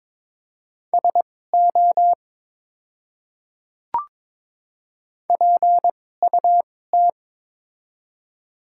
Morse code, heard as a keyed tone, about 22 wpm, spaced as for 16 wpm, 700 Hz. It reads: SO PUT
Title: 2x Character Spacing - Top 200 Words - 22wpm